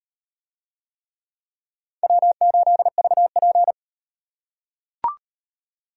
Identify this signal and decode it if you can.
Morse code, heard as a keyed tone, about 38 wpm, 700 Hz. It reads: W8VP